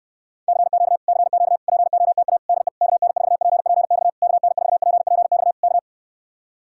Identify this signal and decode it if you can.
Morse code, heard as a keyed tone, about 69 wpm, 700 Hz. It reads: CQ CQ POTA DE KN4PPX KN4PPX K